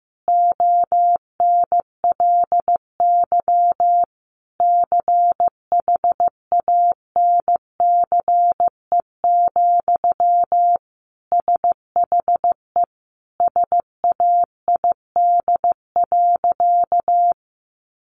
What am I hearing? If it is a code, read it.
Morse code, heard as a keyed tone, about 15 wpm, 700 Hz. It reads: ONLY CHANCE, SHE SAID.
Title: Sentences from Top 800 Words 15wpm